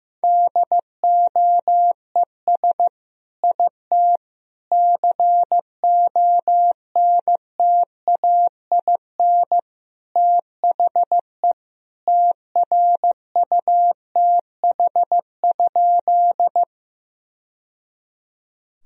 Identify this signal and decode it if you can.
Morse code, heard as a keyed tone, about 15 wpm, 700 Hz. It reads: DOES IT CONTAIN THE TRUTH?
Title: Sentences from Top 1000 Words - Review Part 2 - 15wpm